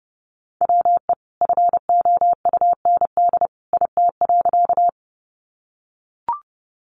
Morse code, heard as a keyed tone, about 30 wpm, 700 Hz. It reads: WE FOUND IT.
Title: Sentences from Top 300 Words 30wpm